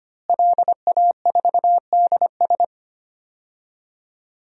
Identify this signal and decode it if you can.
Morse code, heard as a keyed tone, about 25 wpm, 700 Hz. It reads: LA4DS
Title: DXCC-100 Call Signs - 5 Characters - Repeated 3x - 25wpm